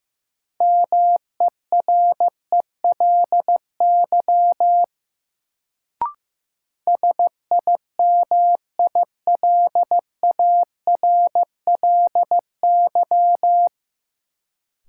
Morse code, heard as a keyed tone, about 15 wpm, 700 Hz. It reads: MERELY SIMILARLY
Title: Sentences from Top 1400 Words - New Word Drill Alt - 15wpm